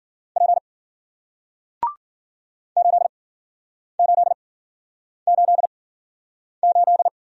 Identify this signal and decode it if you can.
Morse code, heard as a keyed tone, about 52 wpm, spaced as for 9 wpm, 700 Hz. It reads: P 8888